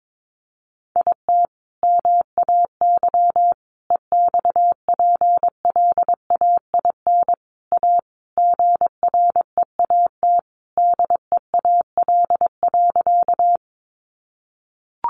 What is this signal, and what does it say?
Morse code, heard as a keyed tone, about 22 wpm, 700 Hz. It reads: IT MAY EXPLAIN A GREAT DEAL.
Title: Sentences from Top 1100 Words 22wpm